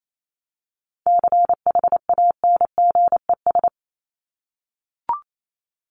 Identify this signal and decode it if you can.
Morse code, heard as a keyed tone, about 28 wpm, 700 Hz. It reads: CHANGES
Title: ES:S Suffix - Top 100 Words with Base Words 28wpm